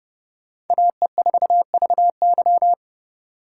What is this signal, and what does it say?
Morse code, heard as a keyed tone, about 28 wpm, 700 Hz. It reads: AE4VY